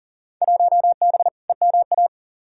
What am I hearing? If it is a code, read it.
Morse code, heard as a keyed tone, about 40 wpm, 700 Hz. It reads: 1B EMA
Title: Field Day Exchanges - 40wpm